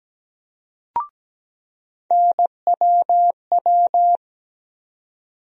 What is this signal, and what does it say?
Morse code, heard as a keyed tone, about 17 wpm, 700 Hz. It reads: NWW
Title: USA Call Sign Postfixes - 3 Letters 17wpm